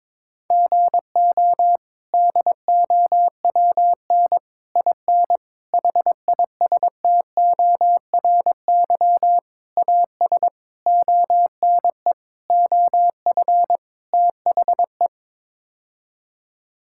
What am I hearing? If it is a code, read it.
Morse code, heard as a keyed tone, about 22 wpm, 700 Hz. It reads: GO DOWN IN HISTORY AS ONE OF THE
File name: practice / 22wpm / Sets of 8 Words - 22wpm